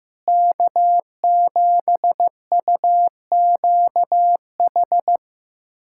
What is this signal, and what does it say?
Morse code, heard as a keyed tone, about 15 wpm, 700 Hz. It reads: K7UQH
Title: USA Callsign DE Callsign - 5 Characters - 15wpm